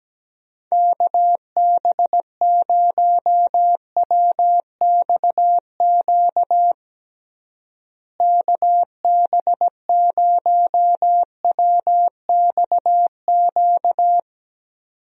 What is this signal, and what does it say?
Morse code, heard as a keyed tone, about 17 wpm, 700 Hz. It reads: KB0WXQ KB0WXQ
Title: USA Call Signs - 6 Characters - Repeated 2x - 17wpm